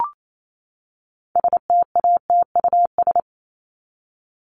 Morse code, heard as a keyed tone, about 28 wpm, 700 Hz. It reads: STATUS
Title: Top 1251-1500 Words 28wpm